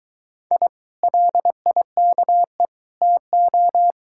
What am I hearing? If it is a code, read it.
Morse code, heard as a keyed tone, about 22 wpm, 700 Hz. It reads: I LIKE TO